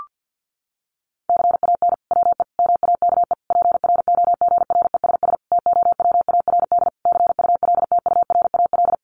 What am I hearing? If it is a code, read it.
Morse code, heard as a keyed tone, about 50 wpm, 700 Hz. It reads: CAN WE MAKE PROGRESS TOWARD CULTURAL
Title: Sentences from Top 1500 Words 50wpm